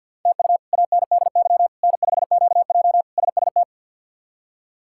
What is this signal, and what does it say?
Morse code, heard as a keyed tone, about 52 wpm, 700 Hz. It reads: TU ANDY N5QJ SST
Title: K1USN SST QSO Practice - 50wpm